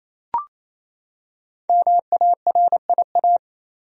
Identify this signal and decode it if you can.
Morse code, heard as a keyed tone, about 28 wpm, 700 Hz. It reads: MARIA
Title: USA Common Names - 28wpm